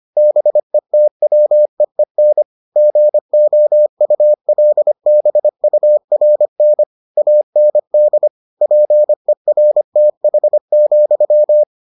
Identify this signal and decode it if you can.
Morse code, heard as a keyed tone, about 25 wpm, 600 Hz. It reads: BETWEEN GOULBURN AND PERTH,